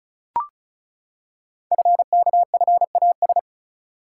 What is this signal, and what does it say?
Morse code, heard as a keyed tone, about 35 wpm, 700 Hz. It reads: FKFAS
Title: Code Groups - 5 Letter:Numbers - 35wpm